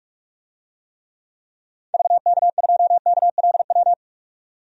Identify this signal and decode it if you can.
Morse code, heard as a keyed tone, about 45 wpm, 700 Hz. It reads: VK2KLW
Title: DXCC-100 Call Signs - 6 Characters - 45wpm